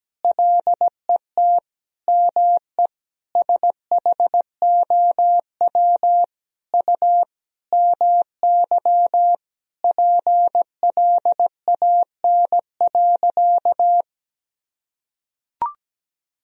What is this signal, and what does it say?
Morse code, heard as a keyed tone, about 17 wpm, 700 Hz. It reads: LET ME SHOW U MY PLAN.
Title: Sentences from Top 500 Words and Abbreviations 17wpm